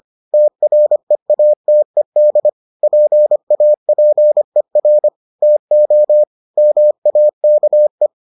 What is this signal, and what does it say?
Morse code, heard as a keyed tone, about 25 wpm, 600 Hz. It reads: TREATED PAPER TO MAKE